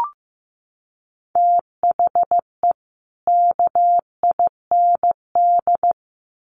Morse code, heard as a keyed tone, about 15 wpm, 700 Hz. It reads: THE KIND
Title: Sets of 3 Words 15wpm